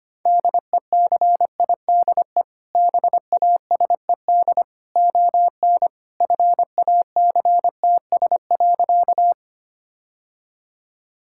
Morse code, heard as a keyed tone, about 25 wpm, 700 Hz. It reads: DECIDE BASED ON FACTS.
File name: practice / 25wpm / Sentences with 2-Word Phrases - 25wpm